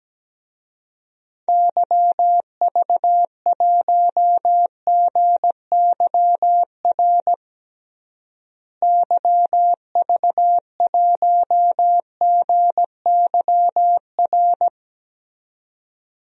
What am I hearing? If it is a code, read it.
Morse code, heard as a keyed tone, about 17 wpm, 700 Hz. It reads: YV1GYR YV1GYR
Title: DXCC-100 Call Signs - 6 Characters - Repeated 2x - 17wpm